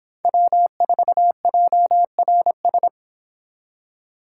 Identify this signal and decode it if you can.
Morse code, heard as a keyed tone, about 26 wpm, 700 Hz. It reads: W4JRS
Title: Speed Racing - USA 1x3 Call Signs 22wpm